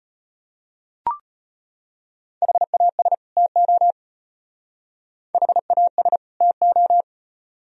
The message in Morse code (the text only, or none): HAS TO HAS TO